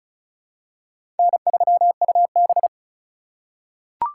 Morse code, heard as a keyed tone, about 34 wpm, 700 Hz. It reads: N3UB
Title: USA Call Signs - 4 Characters - 3x Character Spacing - 35wpm